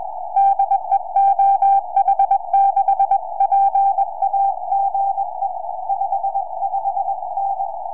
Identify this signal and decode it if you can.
Morse code, heard as a keyed tone, about 21 wpm, 790 Hz. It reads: DE OH6PA GE 55N
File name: oh6pa 4m